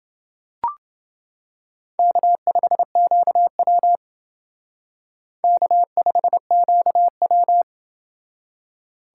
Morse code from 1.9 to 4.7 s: K5QW

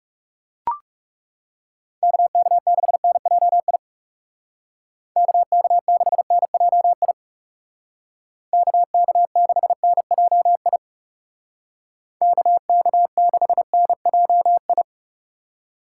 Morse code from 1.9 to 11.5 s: KK6NJI KK6NJI KK6NJI

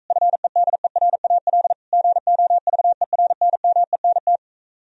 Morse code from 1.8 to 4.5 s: GOVERNMENT

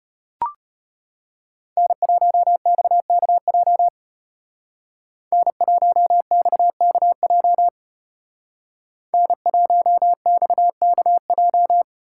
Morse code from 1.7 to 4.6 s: N1XKJ

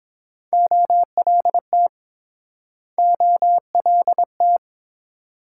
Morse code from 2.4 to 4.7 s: OLT

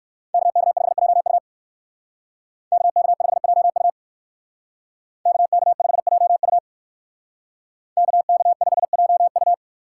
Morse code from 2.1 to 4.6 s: KK5JU

